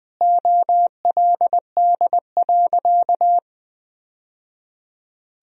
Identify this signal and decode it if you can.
Morse code, heard as a keyed tone, about 20 wpm, 700 Hz. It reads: OLD.
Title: Sentences from Top 1000 Words - Review Part 3 - 20wpm